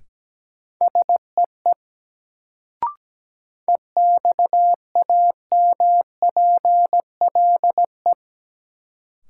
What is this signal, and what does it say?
Morse code, heard as a keyed tone, about 17 wpm, 700 Hz. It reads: SEE EXAMPLE
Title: Sets of 2 Words - New Word Drill - 17wpm